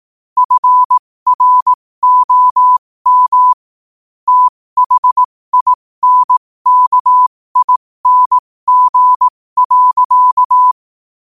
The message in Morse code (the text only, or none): FROM THINKING.